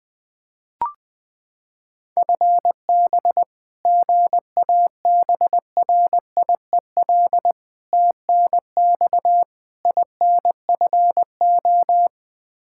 Morse code, heard as a keyed tone, about 20 wpm, 700 Hz. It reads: FB GABRIEL TNX INFO
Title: Practice QSO - Exchange 3 - More Info - 20wpm